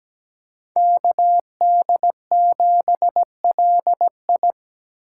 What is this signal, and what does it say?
Morse code, heard as a keyed tone, about 20 wpm, 700 Hz. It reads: KD7LI
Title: Speed Racing - USA 2x2 Call Signs 17wpm